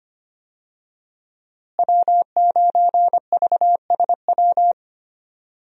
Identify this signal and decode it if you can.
Morse code, heard as a keyed tone, about 25 wpm, 700 Hz. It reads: W9VSW
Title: USA Call Signs - 5 Characters - Repeated 3x - 25wpm